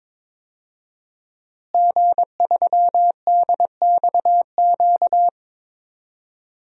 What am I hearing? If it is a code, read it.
Morse code, heard as a keyed tone, about 22 wpm, 700 Hz. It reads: G3DXQ
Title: DXCC-100 Call Signs - 5 Characters - Repeated 3x - 22wpm